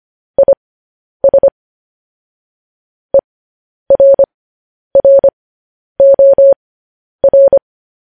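Morse code, heard as a keyed tone, about 25 wpm, spaced as for 10 wpm, 550 Hz. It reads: IS ERROR
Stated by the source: Morse code news headlines sent at 10.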